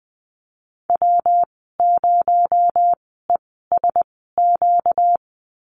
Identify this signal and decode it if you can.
Morse code, heard as a keyed tone, about 20 wpm, 700 Hz. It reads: W 0 E S Q
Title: 238 Lesson 38g - Letter q - Call Signs 20wpm